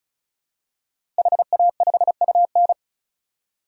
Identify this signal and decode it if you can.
Morse code, heard as a keyed tone, about 36 wpm, 700 Hz. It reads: HA5UN
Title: DXCC-100 Call Signs - 5 Characters - 35wpm